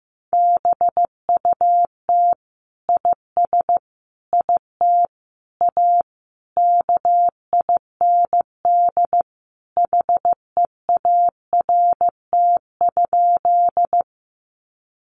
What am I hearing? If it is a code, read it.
Morse code, heard as a keyed tone, about 15 wpm, 700 Hz. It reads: BUT IS IT A KIND HEART?
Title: Sentences from Top 1200 Words 15wpm